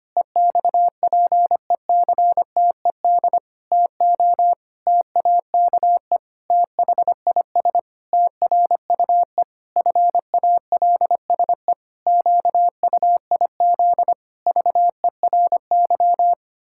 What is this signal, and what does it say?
Morse code, heard as a keyed tone, about 25 wpm, 700 Hz. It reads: EXPECTED TO TAKE THIS TRUE FALSE QUIZ VERY
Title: Pangram Challenge v2 - 25wpm